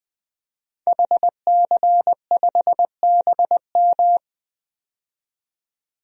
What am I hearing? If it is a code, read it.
Morse code, heard as a keyed tone, about 21 wpm, 700 Hz. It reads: HC5BM
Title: DXCC-100 Call Signs - 5 Characters - 20wpm